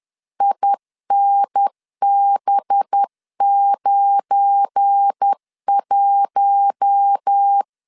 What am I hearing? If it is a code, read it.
Morse code, heard as a keyed tone, about 10 wpm, 800 Hz. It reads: INB91
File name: LEZIONI / 9 2867 10wpm 2733